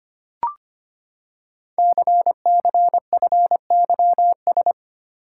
Morse code, heard as a keyed tone, about 25 wpm, 700 Hz. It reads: CCFYS
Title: Code Groups - 5 Letters - 25wpm